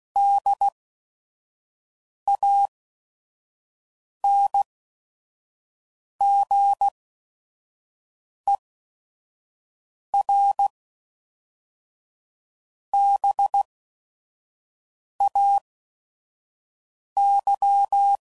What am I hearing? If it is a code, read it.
Morse code, heard as a keyed tone, about 16 wpm, spaced as for 5 wpm, 800 Hz. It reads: DANGER BAY
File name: CW Code Groups @5WPM / fromSM's / Cities